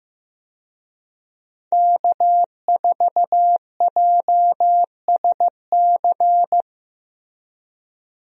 Morse code, heard as a keyed tone, about 15 wpm, 700 Hz. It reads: K4JSC
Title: USA Call Signs - 5 Characters - Repeated 3x - 15wpm